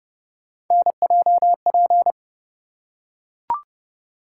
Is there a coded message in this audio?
Yes: Morse code from 0.7 to 2.1 s, heard as a keyed tone, about 29 wpm, 700 Hz. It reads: NJP